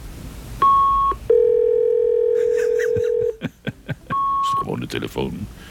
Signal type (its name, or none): fax machine handshake